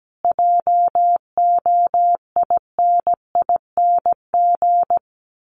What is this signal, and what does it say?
Morse code, heard as a keyed tone, about 17 wpm, 700 Hz. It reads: JOINING
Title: Sets of 5 Words Encore2 17wpm